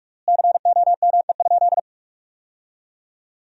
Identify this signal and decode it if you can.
Morse code, heard as a keyed tone, about 46 wpm, 700 Hz. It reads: COME?
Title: 3x Word-Spacing - Sentences from Top 200 Words - 45wpm